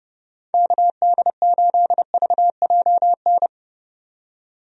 Morse code, heard as a keyed tone, about 30 wpm, 700 Hz. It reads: KD8VJN
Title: USA Call Signs - 6 Characters - 30wpm